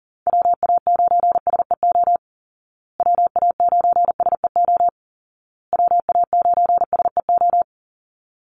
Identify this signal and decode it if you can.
Morse code, heard as a keyed tone, about 40 wpm, 700 Hz. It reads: WA9SEO WA9SEO WA9SEO
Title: Call Signs Repeated 2x - 40wpm